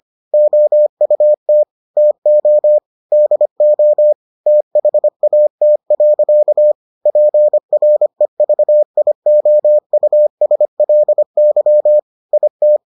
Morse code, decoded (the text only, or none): OUT TO DO THAT. PREVIOUSLY IT